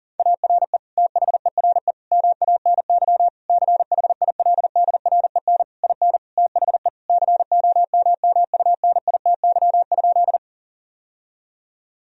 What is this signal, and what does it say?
Morse code, heard as a keyed tone, about 40 wpm, 700 Hz. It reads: ARE THERE MANY CHILDREN IN THE COMMUNITY?
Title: Sentences from Top 1000 Words - Review Part 4 - 40wpm